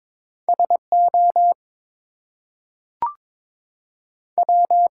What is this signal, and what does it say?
Morse code, heard as a keyed tone, about 22 wpm, 700 Hz. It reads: SO W
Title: Two Letter Words - 22wpm